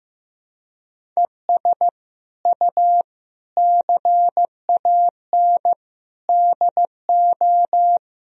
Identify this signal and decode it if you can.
Morse code, heard as a keyed tone, about 15 wpm, 700 Hz. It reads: ES U CAN DO
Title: Sentences from Top 500 Words and Abbreviations 15wpm